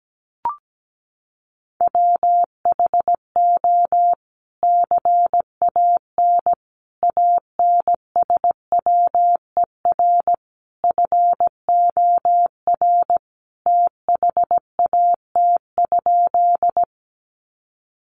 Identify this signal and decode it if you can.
Morse code, heard as a keyed tone, about 17 wpm, 700 Hz. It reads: WHO CAN ANSWER FOR THAT?
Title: Sentences from Top 300 Words 17wpm